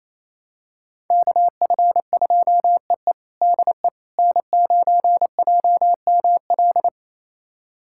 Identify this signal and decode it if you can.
Morse code, heard as a keyed tone, about 28 wpm, 700 Hz. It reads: KF2EE DE N9JML
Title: USA Callsign DE Callsign - 5 Characters - 28wpm